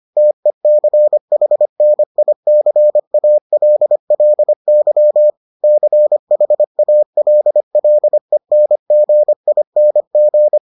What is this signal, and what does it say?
Morse code, heard as a keyed tone, about 25 wpm, 600 Hz. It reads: TECHNICALLY CHALLENGING